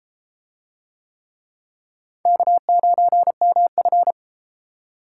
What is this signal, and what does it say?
Morse code, heard as a keyed tone, about 31 wpm, 700 Hz. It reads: K9MF